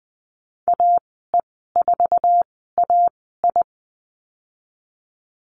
Morse code, heard as a keyed tone, about 20 wpm, spaced as for 15 wpm, 700 Hz. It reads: AE4AI
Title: 055 Lesson 10g - Letter r - Call Signs 20wpm